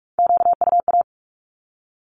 Morse code, heard as a keyed tone, about 46 wpm, 700 Hz. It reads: QUA
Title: Q-Codes with Spoken Meaning - 45wpm